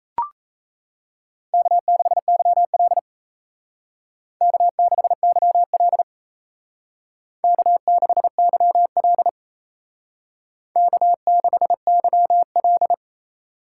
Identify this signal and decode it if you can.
Morse code, heard as a keyed tone, about 36 wpm, 700 Hz. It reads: K6YL K6YL K6YL K6YL